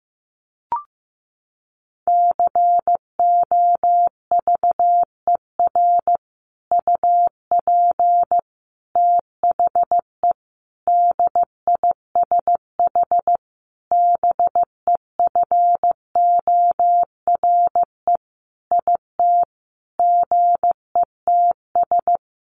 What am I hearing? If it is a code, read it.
Morse code, heard as a keyed tone, about 15 wpm, 700 Hz. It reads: COVER UP THE DISH BEFORE IT GETS
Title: Sentences from Top 1300 Words 15wpm